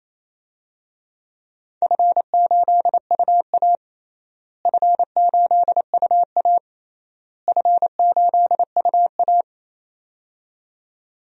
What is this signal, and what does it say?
Morse code, heard as a keyed tone, about 27 wpm, 700 Hz. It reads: F8UA F8UA F8UA
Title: DXCC-100 Call Signs - 4 Characters - Repeated 3x - 28wpm